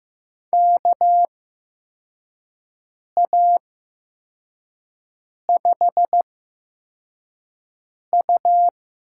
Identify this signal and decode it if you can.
Morse code, heard as a keyed tone, about 15 wpm, spaced as for 4 wpm, 700 Hz. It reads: KA5U
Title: USA Call Signs - 6 Characters - 8x Character Spacing - 15wpm